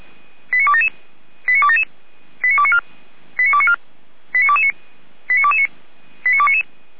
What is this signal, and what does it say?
Five-tone selective call (ZVEI), heard as quick runs of five tones, tones of about 70 ms. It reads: 89277 89277 89294 89294 89229 89229 89299